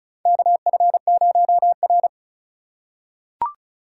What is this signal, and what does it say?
Morse code, heard as a keyed tone, about 36 wpm, 700 Hz. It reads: KF0R
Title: USA Call Signs - 4 Characters - 3x Character Spacing - 35wpm